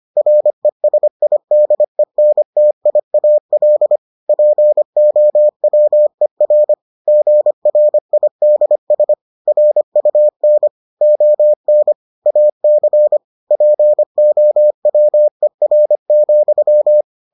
Morse code, heard as a keyed tone, about 25 wpm, 600 Hz. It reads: RESIDENTIAL POWER GRIDS RUN ON AC POWER,